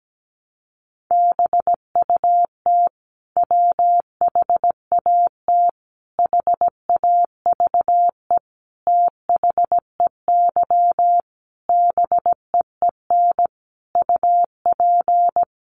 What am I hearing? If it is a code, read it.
Morse code, heard as a keyed tone, about 17 wpm, 700 Hz. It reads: BUT WHAT HAVE THEY BEEN UP
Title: Sentences from Top 100 Words 17wpm